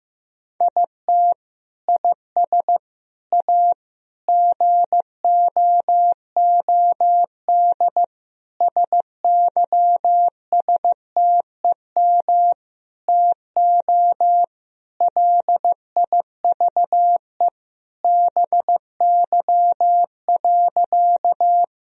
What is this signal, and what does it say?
Morse code, heard as a keyed tone, about 15 wpm, 700 Hz. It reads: IT IS A GOOD SYSTEM TO LIVE BY.
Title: Sentences from Top 1000 Words - Review Part 3 - 15wpm